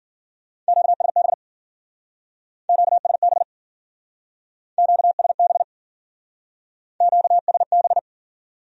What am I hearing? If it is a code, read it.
Morse code, heard as a keyed tone, about 50 wpm, 700 Hz. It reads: QSB QSB QSB QSB
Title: Speed-Racing Q-Codes - 40wpm